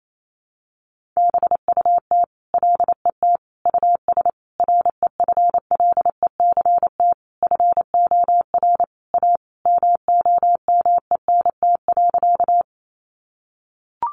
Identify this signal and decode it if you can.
Morse code, heard as a keyed tone, about 28 wpm, 700 Hz. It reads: BUT LET US REFLECT FOR A MOMENT.
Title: Sentences from Top 1500 Words 28wpm